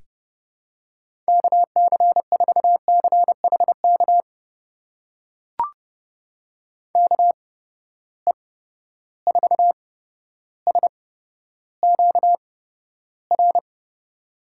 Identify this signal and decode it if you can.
Morse code, heard as a keyed tone, about 32 wpm, spaced as for 13 wpm, 700 Hz. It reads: KC4CHK KE4SQR
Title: USA Call Signs - 6 Characters - 8x Character Spacing - 30wpm